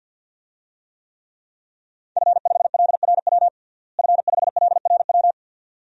Morse code, heard as a keyed tone, about 51 wpm, 700 Hz. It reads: F5LRW F5LRW
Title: DXCC-100 Call Signs - 5 Characters - Repeated 2x - 50wpm